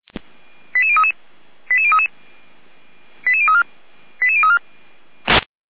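Five-tone selective call (ZVEI), heard as quick runs of five tones, tones of about 70 ms. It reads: 88033 88033 88034 88034
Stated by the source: ZVEI five-tone selective call 88505 received on an SDR.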